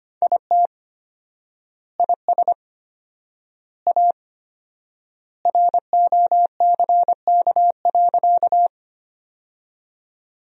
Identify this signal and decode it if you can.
Morse code, heard as a keyed tone, about 25 wpm, 700 Hz. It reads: IT IS A ROCK.
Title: 4x Word-Spacing - Sentences from Top 400 Words - 25wpm